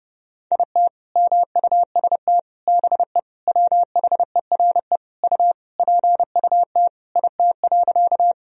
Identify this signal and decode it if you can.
Morse code, heard as a keyed tone, about 32 wpm, 700 Hz. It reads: IT MUST BE WHERE U PUT IT.
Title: Sentences from Top 200 Words and Abbreviations 30wpm